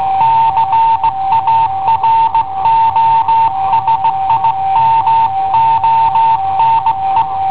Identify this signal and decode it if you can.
Morse code, heard as a keyed tone, about 16 wpm, 900 Hz. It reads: CARO SIMONE
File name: i7ohp / cassetta i7ohp B